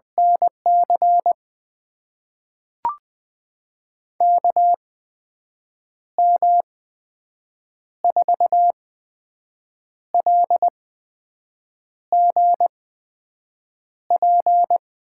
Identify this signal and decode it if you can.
Morse code, heard as a keyed tone, about 20 wpm, spaced as for 6 wpm, 700 Hz. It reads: NC KM4LGP
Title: USA Call Signs - 6 Characters - 8x Character Spacing - 20wpm